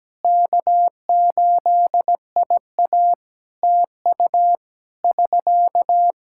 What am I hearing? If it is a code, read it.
Morse code, heard as a keyed tone, about 17 wpm, 700 Hz. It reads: K8IA TU <SK>
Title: K1USN SST QSO Practice 17wpm